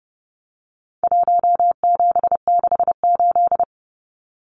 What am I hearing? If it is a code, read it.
Morse code, heard as a keyed tone, about 30 wpm, 700 Hz. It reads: 1768
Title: Numbers - Contest Style - 4-Digit 30wpm